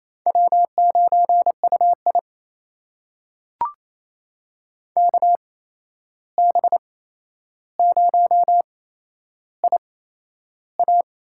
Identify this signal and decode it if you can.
Morse code, heard as a keyed tone, about 28 wpm, spaced as for 11 wpm, 700 Hz. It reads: W9UI KB0IA
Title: USA Call Signs - 8x Character Spacing - 28wpm